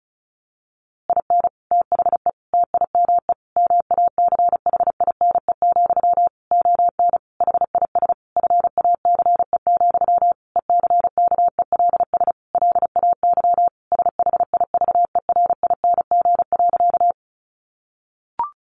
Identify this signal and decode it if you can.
Morse code, heard as a keyed tone, about 34 wpm, 700 Hz. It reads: IN THE TIME MACHINE, ON HIS FACE, ECKELS LAY SHIVERING.
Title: A Sound Of Thunder 35wpm